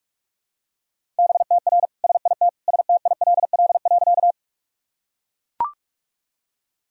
Morse code, heard as a keyed tone, about 42 wpm, 700 Hz. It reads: BTR SIT STILL.
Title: Sentences from Top 500 Words and Abbreviations 45wpm